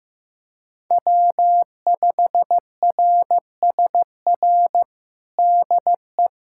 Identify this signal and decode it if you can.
Morse code, heard as a keyed tone, about 15 wpm, 700 Hz. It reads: W5RSR DE
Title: USA Callsign DE Callsign - 5 Characters - 15wpm